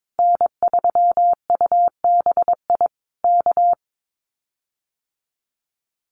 Morse code, heard as a keyed tone, about 22 wpm, 700 Hz. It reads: N3UBI K
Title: Common QSO Phrases Encore 22wpm